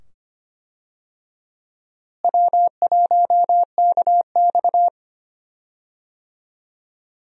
Morse code, heard as a keyed tone, about 25 wpm, 700 Hz. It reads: W1KX
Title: USA Call Signs - 4 Characters - Repeated 2x - 25wpm